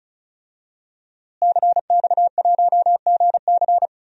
Morse code, heard as a keyed tone, about 36 wpm, 700 Hz. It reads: CX1GC
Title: DXCC-100 Call Signs - 5 Characters - Repeated 3x - 35wpm